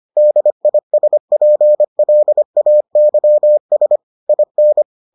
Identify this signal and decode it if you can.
Morse code, heard as a keyed tone, about 25 wpm, 600 Hz. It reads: DISPLAYS IN